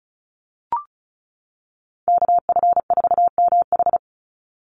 Morse code, heard as a keyed tone, about 34 wpm, 700 Hz. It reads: KF4MH
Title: NA SOTA Chaser Callsigns 35wpm